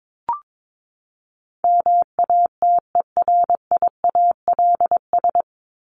Morse code, heard as a keyed tone, about 22 wpm, 700 Hz. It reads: MATERIALS
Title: ES:S Suffix - Top 100 Words with Base Words 22wpm